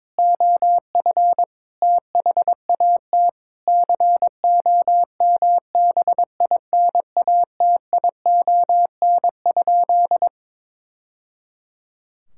Morse code, heard as a keyed tone, about 22 wpm, 700 Hz. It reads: OF THAT COMBINATION?
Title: Sentences from Top 1000 Words - Review Part 6 - 22wpm